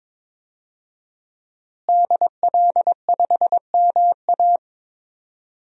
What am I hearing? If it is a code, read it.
Morse code, heard as a keyed tone, about 22 wpm, 700 Hz. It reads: DL5MA
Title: DXCC-100 Call Signs - 5 Characters - Repeated 3x - 22wpm